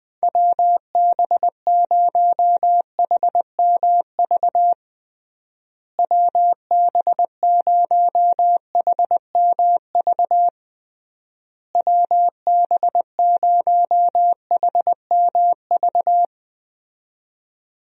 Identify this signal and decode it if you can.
Morse code, heard as a keyed tone, about 21 wpm, 700 Hz. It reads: WB0HMV WB0HMV WB0HMV
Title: USA Call Signs - 6 Characters - Repeated 3x - 20wpm